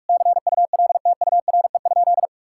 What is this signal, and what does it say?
Morse code, heard as a keyed tone, about 41 wpm, 700 Hz. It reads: CULTURE?
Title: Sentences from Top 1000 Words - Review Part 6 - 45wpm